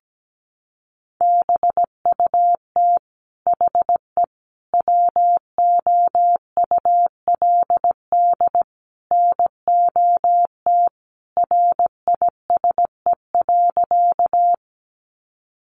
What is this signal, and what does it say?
Morse code, heard as a keyed tone, about 17 wpm, 700 Hz. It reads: BUT HE WOULD NOT RISE.
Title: Sentences from Top 900 Words 17wpm